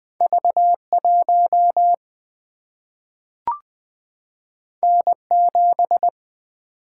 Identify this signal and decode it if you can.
Morse code, heard as a keyed tone, about 20 wpm, 700 Hz. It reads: V1 N7
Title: Single-Letter Single-Number 20wpm